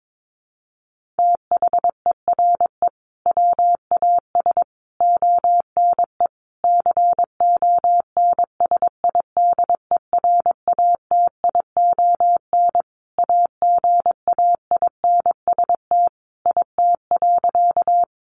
Morse code, heard as a keyed tone, about 22 wpm, 700 Hz. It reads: THERE WAS ONE CONSIDERATION AGAINST IT.
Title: Sentences from Top 1500 Words 22wpm